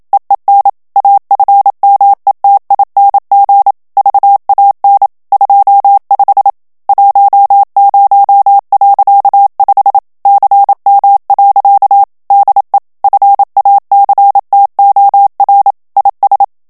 Morse code, heard as a keyed tone, about 27 wpm, 800 Hz. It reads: EEN AFMETING VAN 25 10.5 CM. DE FACTOR IS
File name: HOLLANDAIS texte clair 1 vitesse 26 mots minute